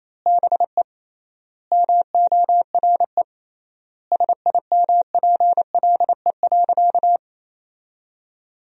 Morse code, heard as a keyed tone, about 28 wpm, 700 Hz. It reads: BE MORE SIMPLE.
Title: 3x Word-Spacing - Sentences from Top 500 Words - 28wpm